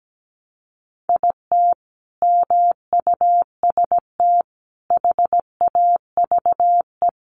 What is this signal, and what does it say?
Morse code, heard as a keyed tone, about 17 wpm, 700 Hz. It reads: IT MUST HAVE
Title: Sentences from Top 1200 Words 17wpm